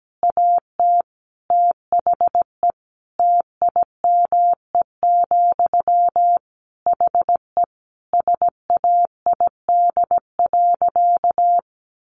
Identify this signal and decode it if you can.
Morse code, heard as a keyed tone, about 17 wpm, 700 Hz. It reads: AT THE TIME, HE SAID.
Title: Sentences from Top 1200 Words 17wpm